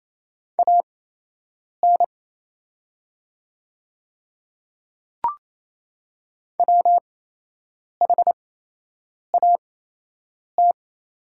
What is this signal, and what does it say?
Morse code, heard as a keyed tone, about 28 wpm, spaced as for 8 wpm, 690 Hz. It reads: AN WHAT
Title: 8x Character Spacing - Top 200 Words - 28wpm